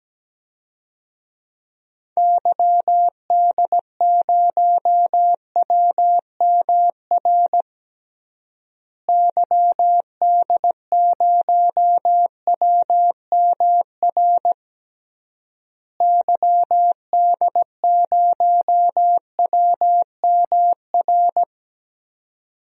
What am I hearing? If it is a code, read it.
Morse code, heard as a keyed tone, about 17 wpm, 700 Hz. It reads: YD0WMR YD0WMR YD0WMR
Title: DXCC-100 Call Signs - 6 Characters - Repeated 3x - 17wpm